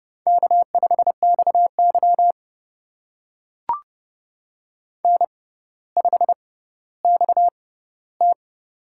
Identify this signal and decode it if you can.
Morse code, heard as a keyed tone, about 30 wpm, spaced as for 15 wpm, 700 Hz. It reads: K5XY N5XT
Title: USA Call Signs - 4 Characters - 6x Character Spacing - 30wpm